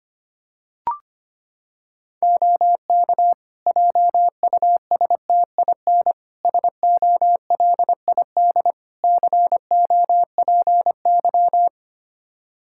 Morse code, heard as a keyed tone, about 25 wpm, 700 Hz. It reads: OK JUSTIN SOLID COPY
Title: Practice QSO - Exchange 3 - More Info - 25wpm